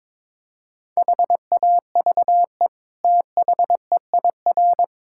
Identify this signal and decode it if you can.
Morse code, heard as a keyed tone, about 22 wpm, 700 Hz. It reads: HAVE THEIR
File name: practice / 22wpm / Sets of 7 Words - 22wpm